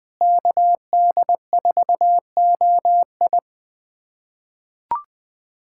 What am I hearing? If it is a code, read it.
Morse code, heard as a keyed tone, about 21 wpm, 700 Hz. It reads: KD4OI
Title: USA Callsign DE Callsign 20wpm